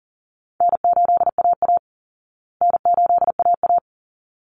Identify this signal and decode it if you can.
Morse code, heard as a keyed tone, about 40 wpm, 700 Hz. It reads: N8AA N8AA
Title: Call Signs Repeated 3x - 40wpm